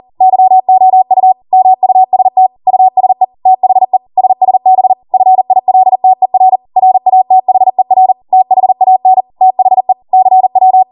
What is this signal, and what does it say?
Morse code, heard as a keyed tone, about 40 wpm, 750 Hz. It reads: YOU MUST USE THE SSB FILTER RATHER THAN THE CW